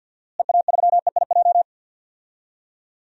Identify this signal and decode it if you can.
Morse code, heard as a keyed tone, about 52 wpm, 700 Hz. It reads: EA3EIJ
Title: DXCC-100 Call Signs - 6 Characters - Repeated 3x - 50wpm